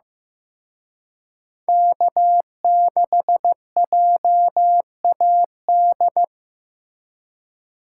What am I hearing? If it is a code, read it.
Morse code, heard as a keyed tone, about 18 wpm, 700 Hz. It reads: K6JAD
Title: Speed Racing - USA 1x3 Call Signs 15wpm